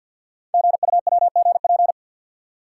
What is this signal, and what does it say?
Morse code, heard as a keyed tone, about 50 wpm, 700 Hz. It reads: GUWGP
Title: Code Groups - 5 Letters - 50wpm